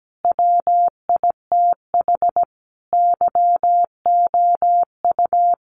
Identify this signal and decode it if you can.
Morse code, heard as a keyed tone, about 17 wpm, 700 Hz. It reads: WITH YOU
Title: Sentences from Top 900 Words 17wpm